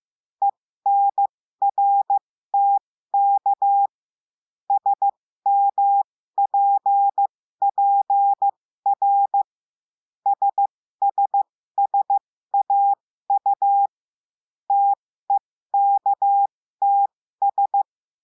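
Morse code, heard as a keyed tone, about 15 wpm, spaced as for 13 wpm, 800 Hz. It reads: ENRTK SMPPR SSSAU TEKTS
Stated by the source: Morse lesson adding the letter T, with five-letter practice groups.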